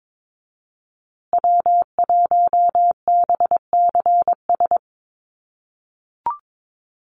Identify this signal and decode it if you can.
Morse code, heard as a keyed tone, about 22 wpm, 700 Hz. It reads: W1BCS
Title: USA Call Signs Repeated 3x 22wpm